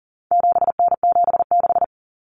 Morse code, heard as a keyed tone, about 40 wpm, 700 Hz. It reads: 7N76
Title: Common Cut Numbers - 4-Digit 40wpm